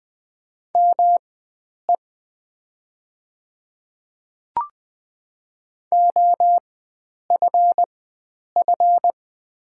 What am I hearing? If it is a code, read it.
Morse code, heard as a keyed tone, about 20 wpm, spaced as for 9 wpm, 700 Hz. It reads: ME OFF
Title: 4x Character Spacing - Top 300 Words - 20wpm